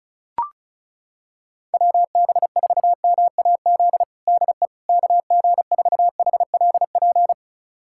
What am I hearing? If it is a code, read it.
Morse code, heard as a keyed tone, about 34 wpm, 700 Hz. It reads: WB4MAZ DE KG4HLP
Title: USA Callsign DE Callsign - 6 Characters - 35wpm